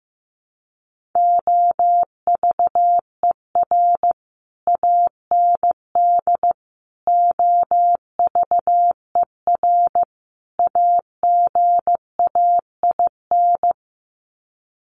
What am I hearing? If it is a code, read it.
Morse code, heard as a keyed tone, about 15 wpm, 700 Hz. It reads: OVER AND OVER AGAIN
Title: Sets of 4 Words 15wpm